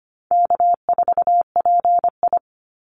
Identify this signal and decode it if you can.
Morse code, heard as a keyed tone, about 25 wpm, 700 Hz. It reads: K4PI
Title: Easy Call Signs Repeated 2x - 25wpm